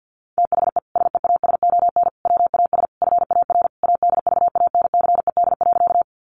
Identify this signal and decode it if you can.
Morse code, heard as a keyed tone, about 50 wpm, 700 Hz. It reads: THE SEASON WAS FAR ADVANCED.
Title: Sentences from Top 1400 Words 50wpm